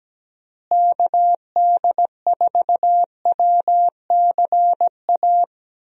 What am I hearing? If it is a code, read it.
Morse code, heard as a keyed tone, about 17 wpm, 700 Hz. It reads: KD4WCA
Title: USA Call Signs - 6 Characters - Repeated 2x - 17wpm